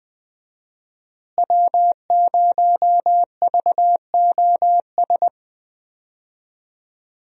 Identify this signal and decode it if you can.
Morse code, heard as a keyed tone, about 20 wpm, 700 Hz. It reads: W0VOS
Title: USA Call Signs - 5 Characters - Repeated 3x - 20wpm